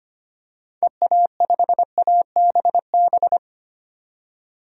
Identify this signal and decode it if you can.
Morse code, heard as a keyed tone, about 25 wpm, 700 Hz. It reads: EA5ABB
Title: DXCC-100 Call Signs - 6 Characters - 25wpm